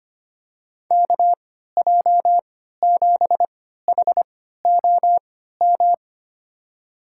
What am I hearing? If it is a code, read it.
Morse code, heard as a keyed tone, about 25 wpm, spaced as for 14 wpm, 700 Hz. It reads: KJ7HOM
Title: USA Call Signs - 6 Characters - 3x Character Spacing - 25wpm